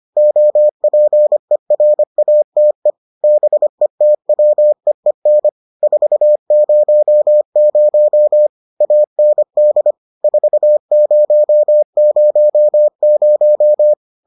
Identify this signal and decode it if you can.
Morse code, heard as a keyed tone, about 25 wpm, 600 Hz. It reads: OPERATE BETWEEN 400 AND 4000